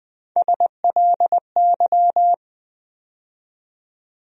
Morse code, heard as a keyed tone, about 20 wpm, 700 Hz. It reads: SLY
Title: USA Call Sign Postfixes - 3 Letters 20wpm